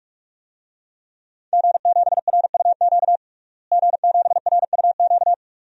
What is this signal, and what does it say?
Morse code, heard as a keyed tone, about 41 wpm, 700 Hz. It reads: G7RUQ G7RUQ
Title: DXCC-100 Call Signs - 5 Characters - Repeated 2x - 45wpm